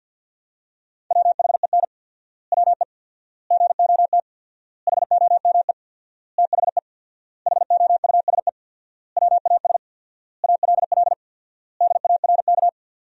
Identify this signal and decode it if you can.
Morse code, heard as a keyed tone, about 51 wpm, 700 Hz. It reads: WHEN WE GOT HOME THE HOUSE WAS ALL DARK
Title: 4x Word-Spacing - Sentences from Top 500 Words - 50wpm